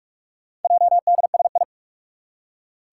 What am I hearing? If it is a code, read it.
Morse code, heard as a keyed tone, about 46 wpm, 700 Hz. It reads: JDSI
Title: Four Challenging Letter-Number Combinations 45wpm